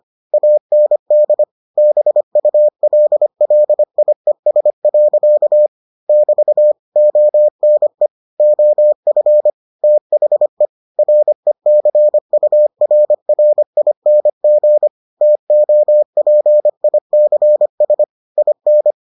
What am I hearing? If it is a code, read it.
Morse code, heard as a keyed tone, about 25 wpm, 600 Hz. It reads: AND BULLIES. <BT> ONE OF THE RECURRING TOPICS IN